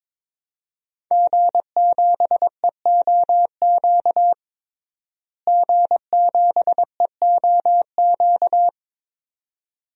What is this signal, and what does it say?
Morse code, heard as a keyed tone, about 22 wpm, 700 Hz. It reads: G7EOQ G7EOQ
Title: DXCC-100 Call Signs - 5 Characters - Repeated 2x - 22wpm